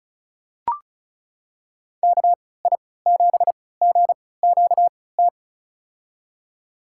Morse code, heard as a keyed tone, about 34 wpm, spaced as for 20 wpm, 700 Hz. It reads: KI7GQT
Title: USA Call Signs - 6 Characters - 3x Character Spacing - 35wpm